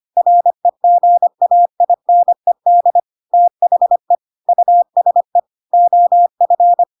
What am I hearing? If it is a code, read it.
Morse code, heard as a keyed tone, about 25 wpm, 700 Hz. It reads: REGAINED THE USE OF
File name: Pinocchio / Pinocchio ch1 25 25 ik1hge